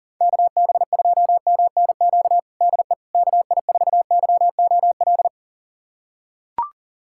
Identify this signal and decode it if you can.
Morse code, heard as a keyed tone, about 40 wpm, 690 Hz. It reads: KB2MNQ DE KI4YOL
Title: USA Callsign DE Callsign - 6 Characters - 40wpm